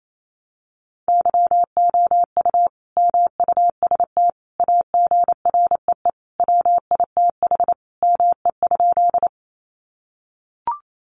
Morse code, heard as a keyed tone, about 28 wpm, 700 Hz. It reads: YOU MUST AGREE WITH ME?
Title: Sentences from Top 1100 Words 28wpm